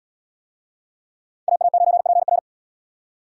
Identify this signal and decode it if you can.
Morse code, heard as a keyed tone, about 77 wpm, 700 Hz. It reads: AA9PR